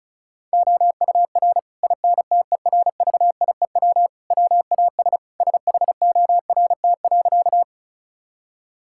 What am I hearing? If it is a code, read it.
Morse code, heard as a keyed tone, about 36 wpm, 700 Hz. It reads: OUR INTERVIEW WAS SHORT.
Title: Sentences from Top 1600 Words 35wpm